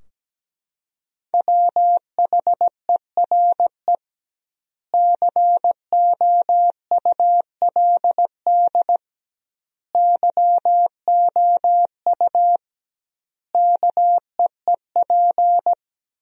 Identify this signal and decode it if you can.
Morse code, heard as a keyed tone, about 17 wpm, 700 Hz. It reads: WHERE COULD YOU KEEP
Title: 2x Word-Spacing - Sentences from Top 300 Words - 17wpm